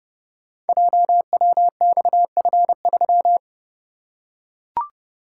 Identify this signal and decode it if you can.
Morse code, heard as a keyed tone, about 30 wpm, 700 Hz. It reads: JWXF3
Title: Code Groups - 5 Letter:Numbers - 30wpm